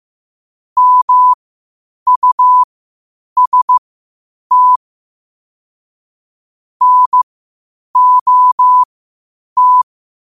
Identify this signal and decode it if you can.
Morse code, heard as a keyed tone, about 15 wpm, spaced as for 8 wpm, 1000 Hz. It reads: MUST NOT
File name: qotdfiles / 8wpm